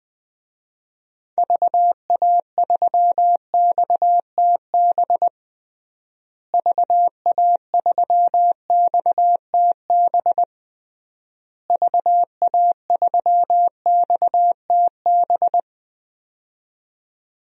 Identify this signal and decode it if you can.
Morse code, heard as a keyed tone, about 20 wpm, 700 Hz. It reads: VA3XTB VA3XTB VA3XTB
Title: DXCC-100 Call Signs - 6 Characters - Repeated 3x - 20wpm